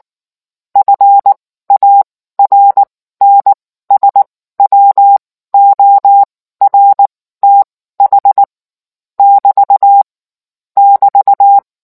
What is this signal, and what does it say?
Morse code, heard as a keyed tone, about 19 wpm, spaced as for 14 wpm, 800 Hz. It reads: FARNSWORTH <BT> <BT>